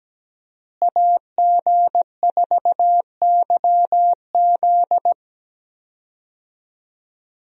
Morse code, heard as a keyed tone, about 17 wpm, 700 Hz. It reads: AG4YZ
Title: USA Call Signs - 5 Characters - Repeated 2x - 17wpm